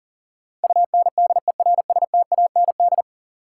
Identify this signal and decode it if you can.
Morse code, heard as a keyed tone, about 40 wpm, 700 Hz. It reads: UNDERSTAND
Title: Sets of 4 Words - New Word Drill - 40wpm